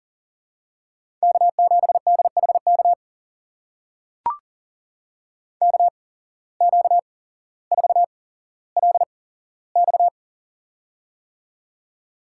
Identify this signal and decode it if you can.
Morse code, heard as a keyed tone, about 42 wpm, spaced as for 17 wpm, 700 Hz. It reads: K7DHK KQ4LX
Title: USA Call Signs - 5 Characters - 8x Character Spacing - 40wpm